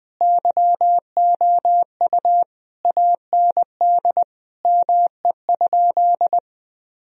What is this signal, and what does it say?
Morse code, heard as a keyed tone, about 21 wpm, 700 Hz. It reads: YOU AND ME?
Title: Sentences from Top 1000 Words - Review Part 3 - 20wpm